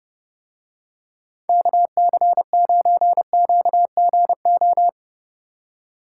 Morse code, heard as a keyed tone, about 29 wpm, 700 Hz. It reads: KC9QGO